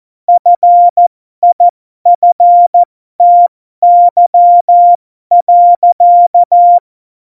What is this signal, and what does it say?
Morse code, heard as a keyed tone, about 14 wpm, spaced as for 12 wpm, 700 Hz. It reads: FIFTY.
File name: Morsekurs / Witze